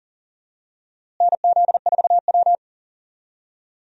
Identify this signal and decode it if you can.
Morse code, heard as a keyed tone, about 40 wpm, 700 Hz. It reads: NZ4W
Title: USA Call Signs - 4 Characters - Repeated 3x - 40wpm